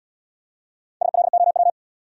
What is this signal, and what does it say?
Morse code, heard as a keyed tone, about 67 wpm, 700 Hz. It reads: SLOW